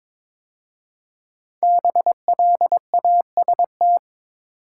Morse code, heard as a keyed tone, about 22 wpm, 700 Hz. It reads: BLAST
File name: 5 Letter Words - First 3 Letters Given - 22wpm